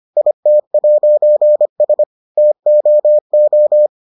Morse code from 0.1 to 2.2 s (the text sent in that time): IT'S